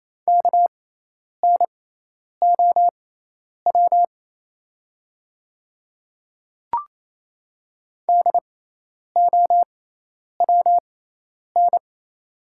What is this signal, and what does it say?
Morse code, heard as a keyed tone, about 28 wpm, spaced as for 10 wpm, 690 Hz. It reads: KNOW DOWN
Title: 6x Character Spacing - Top 100 Words - 28wpm